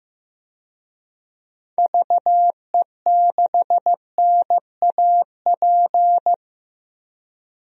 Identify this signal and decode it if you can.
Morse code, heard as a keyed tone, about 15 wpm, 700 Hz. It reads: VE6NAP
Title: DXCC-100 Call Signs - 6 Characters - 15wpm